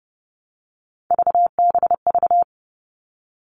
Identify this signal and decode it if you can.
Morse code, heard as a keyed tone, about 30 wpm, 700 Hz. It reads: VBV
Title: All Cut Numbers - 3-Digit 30wpm